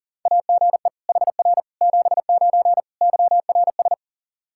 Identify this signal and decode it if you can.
Morse code, heard as a keyed tone, about 40 wpm, 700 Hz. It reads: AGE HR 79 YRS
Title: Practice QSO - Exchange 3 - More Info - 40wpm